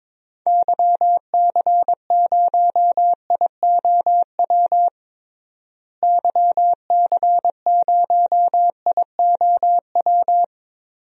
Morse code, heard as a keyed tone, about 21 wpm, 700 Hz. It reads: YC0IOW YC0IOW
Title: DXCC-100 Call Signs - 6 Characters - Repeated 3x - 22wpm